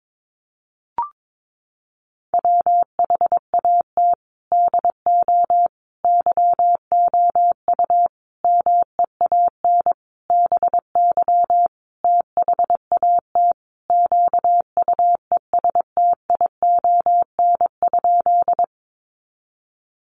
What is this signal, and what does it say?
Morse code, heard as a keyed tone, about 22 wpm, 700 Hz. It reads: WHAT DO YOU MEAN BY THAT QUESTION?
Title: Sentences from Top 400 Words 22wpm